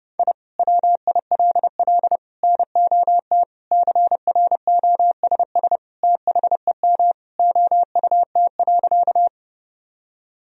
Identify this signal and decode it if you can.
Morse code, heard as a keyed tone, about 30 wpm, 700 Hz. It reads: I WILL NOT CROSS THEM OUT.
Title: Sentences from Top 1000 Words - Review Part 1 - 30wpm